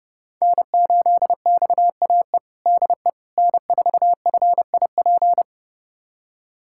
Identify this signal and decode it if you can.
Morse code, heard as a keyed tone, about 32 wpm, 700 Hz. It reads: N8XAE DE N4FIP
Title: USA Callsign DE Callsign - 5 Characters - 30wpm